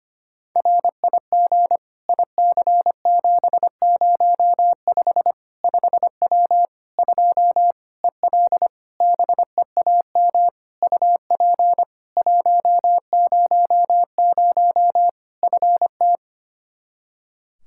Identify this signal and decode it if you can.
Morse code, heard as a keyed tone, about 25 wpm, 700 Hz. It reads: RIG IC705 5W 2 EL BEAM UP 100 FT
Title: K1USN SST QSO Practice 25wpm